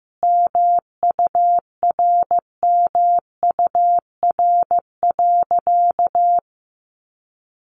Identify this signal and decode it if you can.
Morse code, heard as a keyed tone, about 15 wpm, 700 Hz. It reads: MURMUR.
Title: A Sound Of Thunder 15wpm